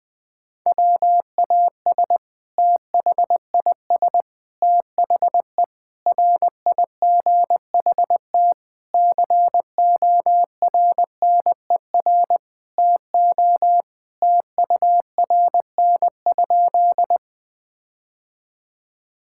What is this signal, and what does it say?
Morse code, heard as a keyed tone, about 21 wpm, 700 Hz. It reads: WAS THIS THE RIGHT CORNER TO TURN?
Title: Sentences from Top 1600 Words 20wpm